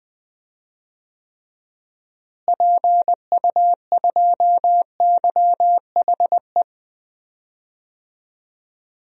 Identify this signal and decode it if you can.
Morse code, heard as a keyed tone, about 20 wpm, 700 Hz. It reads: PU2YHE
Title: DXCC-100 Call Signs - 6 Characters - Repeated 2x - 20wpm